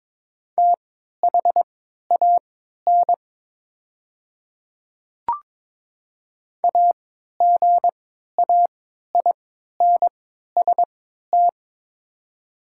Morse code, heard as a keyed tone, about 22 wpm, spaced as for 13 wpm, 700 Hz. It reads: THAN AGAINST
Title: 3x Character Spacing - Top 500 Words - 22wpm